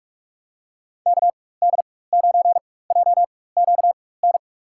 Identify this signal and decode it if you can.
Morse code, heard as a keyed tone, about 45 wpm, spaced as for 21 wpm, 700 Hz. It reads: KD9JQN
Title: USA Call Signs - 6 Characters - 4x Character Spacing - 45wpm